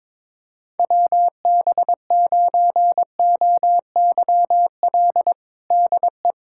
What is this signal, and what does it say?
Morse code, heard as a keyed tone, about 21 wpm, 700 Hz. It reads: WB9OYL DE
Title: USA Callsign DE Callsign - 6 Characters - 22wpm